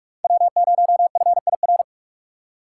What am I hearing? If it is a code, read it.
Morse code, heard as a keyed tone, about 44 wpm, 700 Hz. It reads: W0FIR